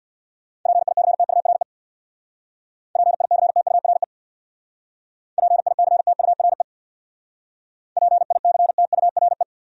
Morse code, heard as a keyed tone, about 60 wpm, 700 Hz. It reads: PICTURE PICTURE PICTURE PICTURE
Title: Speed-Racing Top 300 Words 50wpm